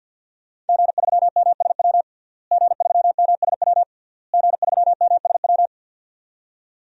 Morse code, heard as a keyed tone, about 50 wpm, 700 Hz. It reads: G3MSW G3MSW G3MSW
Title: DXCC-100 Call Signs - 5 Characters - Repeated 3x - 50wpm